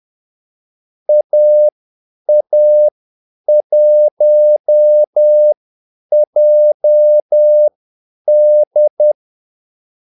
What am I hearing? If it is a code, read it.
Morse code, heard as a keyed tone, about 10 wpm, spaced as for 8 wpm, 600 Hz. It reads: AA1JD
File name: Callsigns-10-8wpm